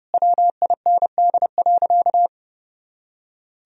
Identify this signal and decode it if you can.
Morse code, heard as a keyed tone, about 30 wpm, 700 Hz. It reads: WIND.
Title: Sentences from Top 400 Words and Abbreviations 30wpm